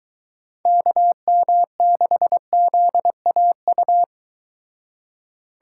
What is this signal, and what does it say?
Morse code, heard as a keyed tone, about 23 wpm, 700 Hz. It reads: KM6ZAU